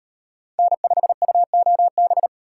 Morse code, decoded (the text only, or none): N5UOB